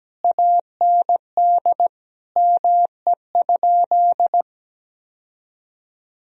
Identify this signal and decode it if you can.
Morse code, heard as a keyed tone, about 17 wpm, 700 Hz. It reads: AND ME?
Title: Sentences from Top 1000 Words - Review Part 2 - 17wpm